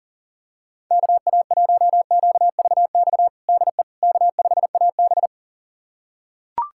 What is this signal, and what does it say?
Morse code, heard as a keyed tone, about 39 wpm, 700 Hz. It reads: KA1QVX DE K5AB
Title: USA Callsign DE Callsign 40wpm